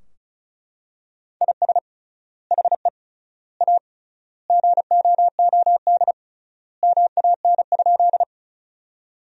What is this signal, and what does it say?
Morse code, heard as a keyed tone, about 36 wpm, 700 Hz. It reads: IS HE A GOOD MAN?
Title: 3x Word-Spacing - Sentences from Top 200 Words - 35wpm